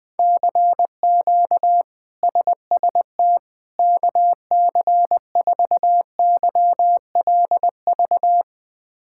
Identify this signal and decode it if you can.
Morse code, heard as a keyed tone, about 20 wpm, 700 Hz. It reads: CQ SST KC4YLV
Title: K1USN SST QSO Practice - 20wpm